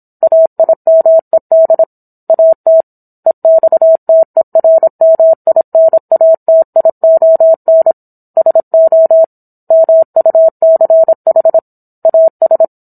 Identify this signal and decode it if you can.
Morse code, heard as a keyed tone, about 27 wpm, 650 Hz. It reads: AIMED AT EXTERMINATION SO MUCH AS